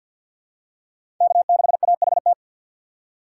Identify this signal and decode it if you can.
Morse code, heard as a keyed tone, about 53 wpm, 700 Hz. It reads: K6AHT
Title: USA Call Signs - 5 Characters - 4x Character Spacing - 50wpm